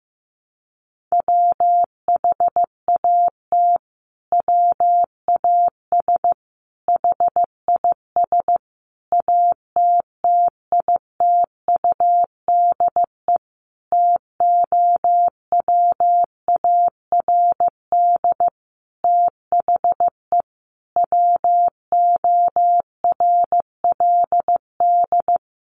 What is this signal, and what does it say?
Morse code, heard as a keyed tone, about 15 wpm, 700 Hz. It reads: WHAT WAS HIS ATTITUDE TOWARD THE WORLD
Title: Sentences from Top 1300 Words 15wpm